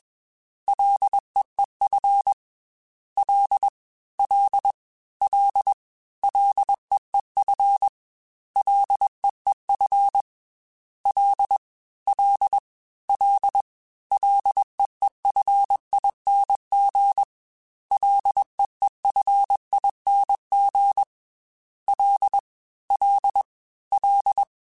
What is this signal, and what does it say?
Morse code, heard as a keyed tone, about 21 wpm, 800 Hz. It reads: LEEF L L L LEEF LEEF L L L LEEFING LEEFING L L L
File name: Ls / Single L m3s / Single L Prefixe 3x2 Twice @20